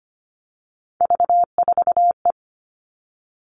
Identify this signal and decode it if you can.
Morse code, heard as a keyed tone, about 25 wpm, 700 Hz. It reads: V4E
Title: Three Challenging Letter-Number Combinations 25wpm